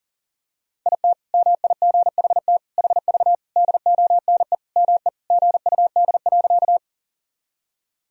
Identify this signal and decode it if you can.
Morse code, heard as a keyed tone, about 40 wpm, 700 Hz. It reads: IT MIGHT HV DONE ME GUD.
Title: Sentences from Top 500 Words and Abbreviations 40wpm